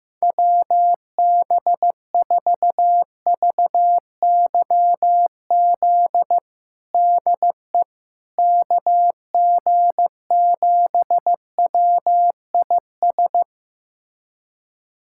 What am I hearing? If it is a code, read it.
Morse code, heard as a keyed tone, about 15 wpm, 700 Hz. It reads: WB4VYZ DE KG7WIS
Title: USA Callsign DE Callsign 15wpm